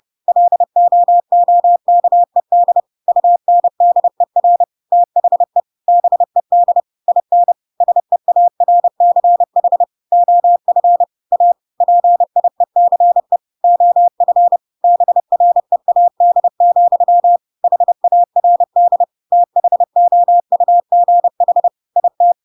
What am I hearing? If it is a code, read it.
Morse code, heard as a keyed tone, about 30 wpm, 700 Hz. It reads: LOOKED UNDER THE BED IN SEARCH OF A PIECE OF BREAD, HARD THOUGH IT
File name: Pinocchio / Pinocchio ch5 30 30 ik1hge